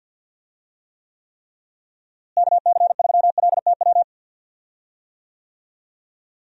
Morse code, heard as a keyed tone, about 52 wpm, 700 Hz. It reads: KC3LTW
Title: USA Call Signs - 6 Characters - Repeated 3x - 50wpm